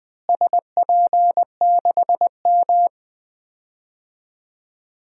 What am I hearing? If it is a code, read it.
Morse code, heard as a keyed tone, about 21 wpm, 700 Hz. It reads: SP6M
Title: DXCC-100 Call Signs - 4 Characters - Repeated 2x - 20wpm